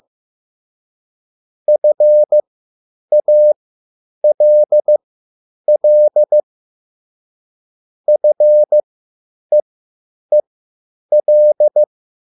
Morse code, heard as a keyed tone, about 15 wpm, spaced as for 9 wpm, 600 Hz. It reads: FALL FEEL
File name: QSL / CW lesson